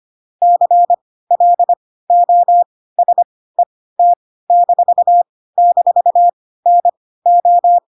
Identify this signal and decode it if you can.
Morse code, heard as a keyed tone, about 25 wpm, spaced as for 16 wpm, 700 Hz. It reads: CLOSET--NO
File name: Pinocchio / Pinocchio ch1 10 25 ik1hge